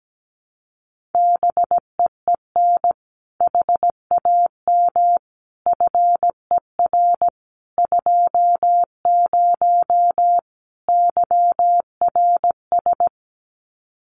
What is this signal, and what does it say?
Morse code, heard as a keyed tone, about 17 wpm, 700 Hz. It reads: BEEN HAM FER 20 YRS
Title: Common QSO Phrases 17wpm